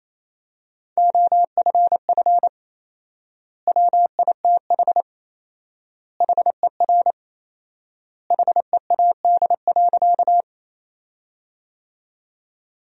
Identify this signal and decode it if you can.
Morse code, heard as a keyed tone, about 28 wpm, 700 Hz. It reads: OFF WITH HER HEAD.
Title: 4x Word-Spacing - Sentences from Top 300 Words - 28wpm